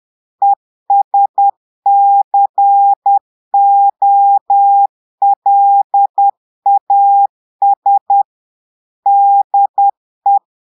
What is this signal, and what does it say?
Morse code, heard as a keyed tone, about 10 wpm, 800 Hz. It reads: ESCOLAS DE
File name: Portugais - texte en clair 1 - vitesse - 10 mots minute